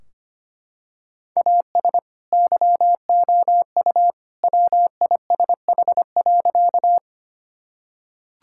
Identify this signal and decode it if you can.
Morse code, heard as a keyed tone, about 25 wpm, 700 Hz. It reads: AS YOU WISH.
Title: Sentences from Top 1600 Words 25wpm